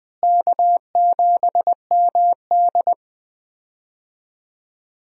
Morse code, heard as a keyed tone, about 20 wpm, 700 Hz. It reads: K7MD
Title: USA Call Signs - 4 Characters - Repeated 2x - 20wpm